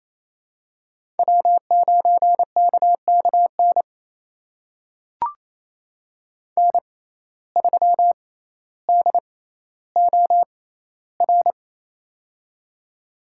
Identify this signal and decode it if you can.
Morse code, heard as a keyed tone, about 28 wpm, spaced as for 14 wpm, 700 Hz. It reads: W9KKN N3DOR
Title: USA Call Signs - 5 Characters - 6x Character Spacing - 28wpm